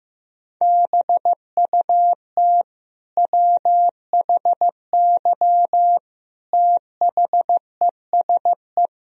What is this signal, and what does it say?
Morse code, heard as a keyed tone, about 15 wpm, 700 Hz. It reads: BUT WHY THESE
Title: Sentences from Top 1000 Words - Review Part 1 - 15wpm